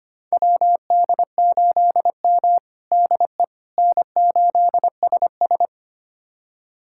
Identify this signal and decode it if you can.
Morse code, heard as a keyed tone, about 25 wpm, 700 Hz. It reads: WD8M DE N8SS
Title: USA Callsign DE Callsign - 4 Characters - 25wpm